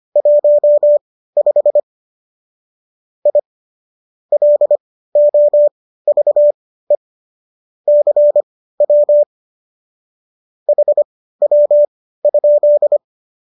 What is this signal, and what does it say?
Morse code, heard as a keyed tone, about 25 wpm, spaced as for 15 wpm, 590 Hz. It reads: 15 I LOVE CW HW?